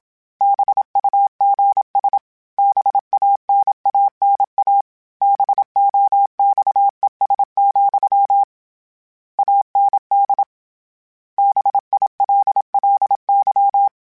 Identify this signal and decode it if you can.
Morse code, heard as a keyed tone, about 27 wpm, 800 Hz. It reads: BUGS BANANA BOXES, AND BILLY